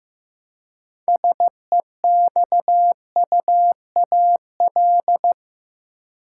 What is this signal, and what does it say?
Morse code, heard as a keyed tone, about 15 wpm, 700 Hz. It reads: SEXUAL
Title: Six Letter Words - 15wpm